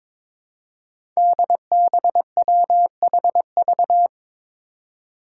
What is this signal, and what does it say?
Morse code, heard as a keyed tone, about 22 wpm, 700 Hz. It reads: DBWHV
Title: Five Challenging Letter-Number Combinations 22wpm